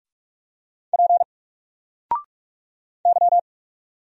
Morse code, heard as a keyed tone, about 44 wpm, 700 Hz. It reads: P Y
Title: Single Character (Letter:Number:Punctuation:Prosign) 45wpm